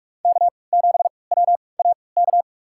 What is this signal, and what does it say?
Morse code, heard as a keyed tone, about 46 wpm, spaced as for 26 wpm, 700 Hz. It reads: K7WAK
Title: USA Call Signs - 3x Character Spacing - 45wpm